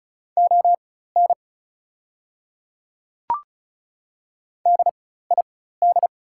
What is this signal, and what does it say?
Morse code, heard as a keyed tone, about 36 wpm, spaced as for 16 wpm, 690 Hz. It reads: ON DID
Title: 4x Character Spacing - Top 100 Words - 35wpm